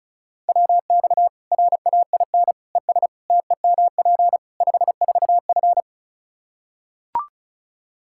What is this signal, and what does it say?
Morse code, heard as a keyed tone, about 36 wpm, 700 Hz. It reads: WX RAIN ES TEMP 54F
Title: Practice QSO - Exchange 2 - Info - 35wpm